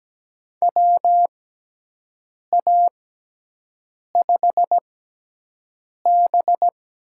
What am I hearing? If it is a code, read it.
Morse code, heard as a keyed tone, about 17 wpm, spaced as for 6 wpm, 700 Hz. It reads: WA5B
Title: USA Call Signs - 6 Characters - 6x Character Spacing - 17wpm